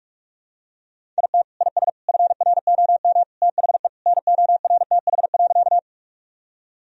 Morse code, heard as a keyed tone, about 45 wpm, 700 Hz. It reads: IT IS FROM THE NORTH.
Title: Sentences from Top 1000 Words - Review Part 1 - 45wpm